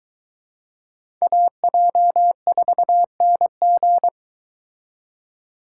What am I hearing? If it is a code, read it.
Morse code, heard as a keyed tone, about 22 wpm, 700 Hz. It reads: AJ4NG